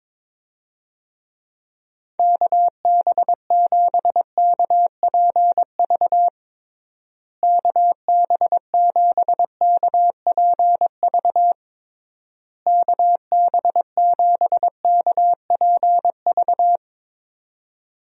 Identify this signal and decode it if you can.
Morse code, heard as a keyed tone, about 22 wpm, 700 Hz. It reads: KB7KPV KB7KPV KB7KPV
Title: USA Call Signs - 6 Characters - Repeated 3x - 22wpm